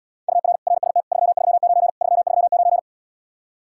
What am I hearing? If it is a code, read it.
Morse code, heard as a keyed tone, about 79 wpm, 700 Hz. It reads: UR RST 339 339